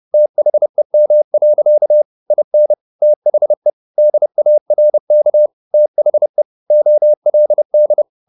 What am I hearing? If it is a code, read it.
Morse code, heard as a keyed tone, about 30 wpm, 600 Hz. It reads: THEM. IN THE DARK THE OLD
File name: the old man and the sea / the old man and the sea 1 30wpm